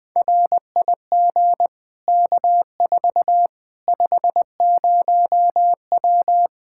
Morse code, heard as a keyed tone, about 21 wpm, 700 Hz. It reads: RIG K4 50W
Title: Practice QSO - Exchange 2 - Info - 20wpm